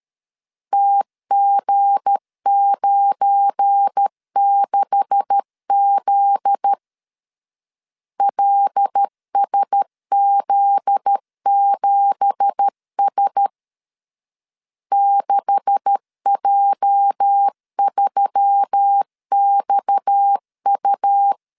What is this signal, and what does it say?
Morse code, heard as a keyed tone, about 13 wpm, 800 Hz. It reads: TG96Z LSZ7S 6J3XU
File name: CW / DET / 1 9 1500C S A 12wpm 3800